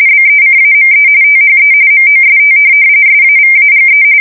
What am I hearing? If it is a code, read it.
RTTY, about 45.45 baud. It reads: BECAME THE MOST SAVAGE
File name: fsk170 / USS-New-Jersey